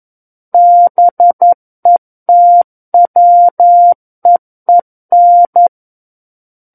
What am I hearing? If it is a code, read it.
Morse code, heard as a keyed tone, about 11 wpm, 700 Hz. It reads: BETWEEN
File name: 11 WPM